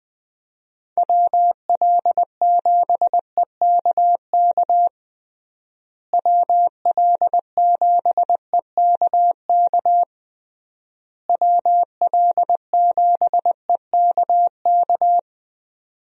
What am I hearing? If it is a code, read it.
Morse code, heard as a keyed tone, about 20 wpm, 700 Hz. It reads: WL7EKK WL7EKK WL7EKK
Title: USA Call Signs - 6 Characters - Repeated 3x - 20wpm